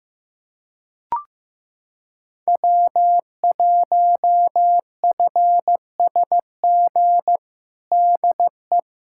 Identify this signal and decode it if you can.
Morse code, heard as a keyed tone, about 15 wpm, 700 Hz. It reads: W1FSG DE
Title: USA Callsign DE Callsign - 5 Characters - 15wpm